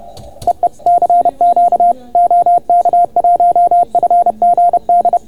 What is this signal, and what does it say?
Morse code, heard as a keyed tone, about 30 wpm, 690 Hz. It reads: EE CQ OK1FGD